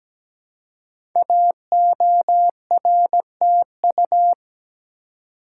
Morse code, heard as a keyed tone, about 17 wpm, 700 Hz. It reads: AORTU
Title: Code Groups - 5 Letters - 17wpm